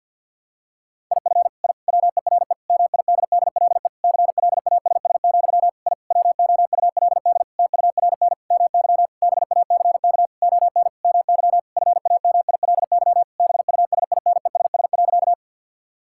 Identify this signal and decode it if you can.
Morse code, heard as a keyed tone, about 52 wpm, 700 Hz. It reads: IF I WERE MIDDLE CLASS, I WOULD TURN MY BACK ON MY FAMILY BUSINESS.
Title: Sentences from Top 1000 Words - Review Part 6 - 50wpm